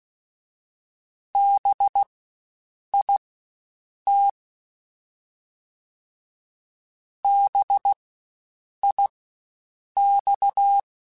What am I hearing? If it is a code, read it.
Morse code, heard as a keyed tone, about 16 wpm, spaced as for 7 wpm, 800 Hz. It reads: BIT BIX
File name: Spaced Alphas / Spaced Alphas mp3s / Bs sp @15